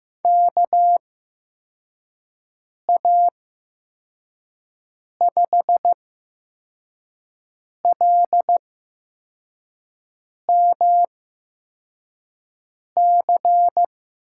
Morse code, heard as a keyed tone, about 15 wpm, spaced as for 4 wpm, 700 Hz. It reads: KA5LMC
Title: USA Call Signs - 6 Characters - 8x Character Spacing - 15wpm